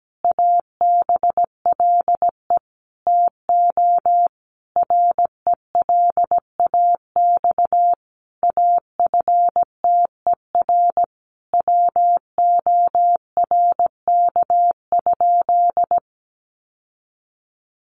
Morse code, heard as a keyed tone, about 17 wpm, 700 Hz. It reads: ABLE TO RELAX AFTER WORK?
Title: Sentences from Top 1400 Words 17wpm